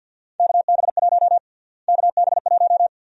Morse code, heard as a keyed tone, about 52 wpm, 700 Hz. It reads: KB1 KB1
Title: USA Call Sign Prefixes - Repeated 2x - 50wpm